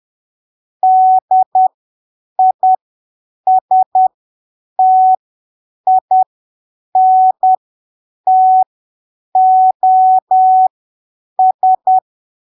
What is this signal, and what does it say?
Morse code, heard as a keyed tone, about 10 wpm, spaced as for 7 wpm, 750 Hz. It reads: DISTINTOS
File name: arquivos / texto / texto 22 10ppm